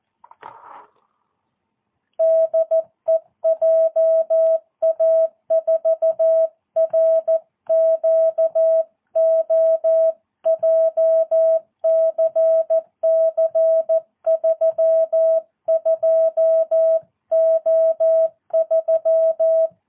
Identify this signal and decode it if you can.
Morse code, heard as a keyed tone, about 14 wpm, 650 Hz. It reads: DE JA4RQO JCC 32O3
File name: jh8cbh / 2022cw